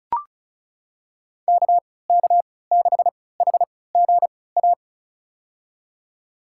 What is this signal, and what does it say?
Morse code, heard as a keyed tone, about 36 wpm, spaced as for 20 wpm, 700 Hz. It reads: KK6HGA
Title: USA Call Signs - 3x Character Spacing - 35wpm